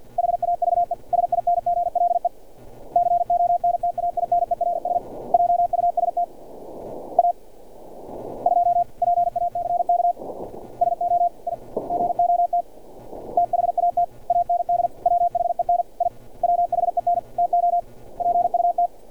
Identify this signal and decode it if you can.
Morse code, heard as a keyed tone, about 50 wpm, 700 Hz. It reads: HAVE SINGLE CONTAINERS JUST A 2 PACK SO I GOT THAT AND WHEN I WHEN TO PUT
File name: archived / CRSnet